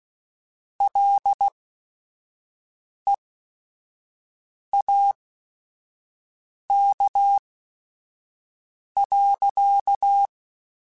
Morse code, heard as a keyed tone, about 16 wpm, spaced as for 5 wpm, 800 Hz. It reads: LEAK.
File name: Chuckles / Chuckles-mp3s / Guaranteed @5